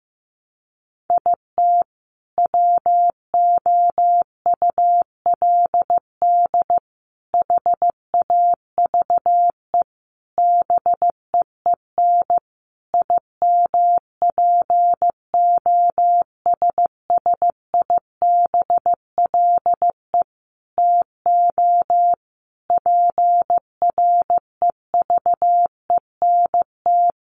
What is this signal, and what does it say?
Morse code, heard as a keyed tone, about 15 wpm, 700 Hz. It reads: IT WOULD HAVE BEEN IMPOSSIBLE TO PREVENT
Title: Sentences from Top 1100 Words 15wpm